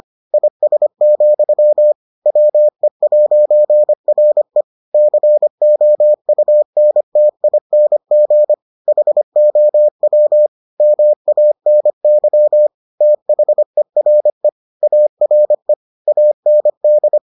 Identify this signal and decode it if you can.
Morse code, heard as a keyed tone, about 25 wpm, 600 Hz. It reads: IS, WE'RE COUNTING HOW MANY THERE ARE AND